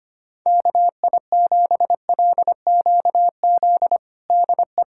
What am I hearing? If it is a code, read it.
Morse code, heard as a keyed tone, about 25 wpm, 700 Hz. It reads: KI7LQZ DE
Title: USA Callsign DE Callsign 25wpm